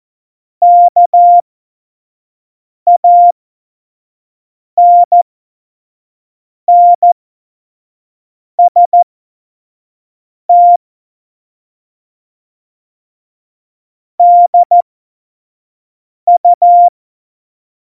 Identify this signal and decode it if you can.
Morse code, heard as a keyed tone, about 14 wpm, spaced as for 5 wpm, 700 Hz. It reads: KANNST DU